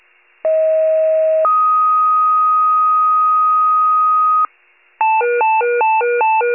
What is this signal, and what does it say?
Two-tone page, heard as a one-second tone, then a three-second tone, about 640 then 1250 Hz. It is heard from 0.4 to 4.5 s.
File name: pager / 38 FIRE / wcalert